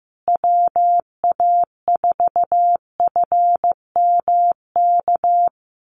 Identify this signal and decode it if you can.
Morse code, heard as a keyed tone, about 15 wpm, 700 Hz. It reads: WA4FMK
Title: USA Call Signs Repeated 3x 15wpm